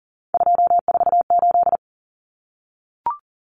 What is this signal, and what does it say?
Morse code, heard as a keyed tone, about 40 wpm, 700 Hz. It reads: 248
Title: Numbers - 3-Digit 40wpm